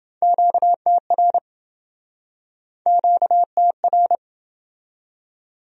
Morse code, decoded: QTR QTR

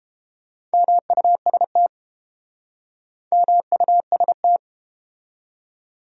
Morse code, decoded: MUST MUST

